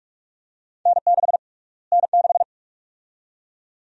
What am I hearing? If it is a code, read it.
Morse code, heard as a keyed tone, about 46 wpm, 700 Hz. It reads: N6 N6
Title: USA Call Sign Prefixes - Repeated 2x - 45wpm